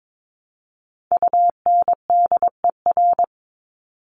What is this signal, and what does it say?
Morse code, heard as a keyed tone, about 22 wpm, 700 Hz. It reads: UNDER
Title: Top 200 Words 22wpm